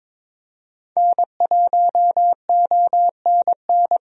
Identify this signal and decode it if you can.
Morse code, heard as a keyed tone, about 22 wpm, 700 Hz. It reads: N1ONN
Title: USA Call Signs - 5 Characters - 22wpm